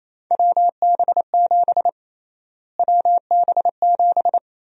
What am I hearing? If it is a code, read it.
Morse code, heard as a keyed tone, about 28 wpm, 700 Hz. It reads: WB7 WB7
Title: USA Call Sign Prefixes - Repeated 2x - 28wpm